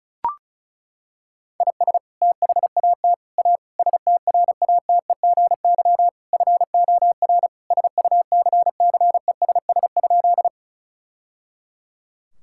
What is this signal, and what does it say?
Morse code, heard as a keyed tone, about 36 wpm, 700 Hz. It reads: IS THAT A STRATEGY FOR SUCCESS?
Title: Sentences from Top 1000 Words - Review Part 5 - 35wpm